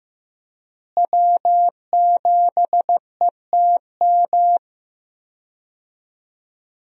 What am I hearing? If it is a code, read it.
Morse code, heard as a keyed tone, about 15 wpm, 700 Hz. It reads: W7ETM
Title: USA Call Signs - 5 Characters - Repeated 2x - 15wpm